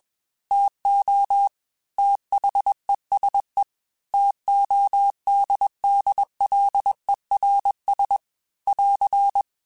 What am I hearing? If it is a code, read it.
Morse code, heard as a keyed tone, about 20 wpm, 800 Hz. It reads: TO THESE TODDLERS <AR>
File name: T Runs / T Runs mp3s / T Runs @20